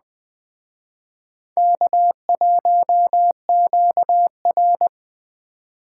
Morse code, heard as a keyed tone, about 20 wpm, 700 Hz. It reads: K1QR
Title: Speed Racing - USA 1x2 Call Signs 17wpm